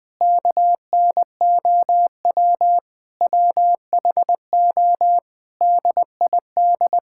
Morse code, heard as a keyed tone, about 21 wpm, 700 Hz. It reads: KNOW WHO DID
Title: Sentences from Top 1000 Words - Review Part 6 - 20wpm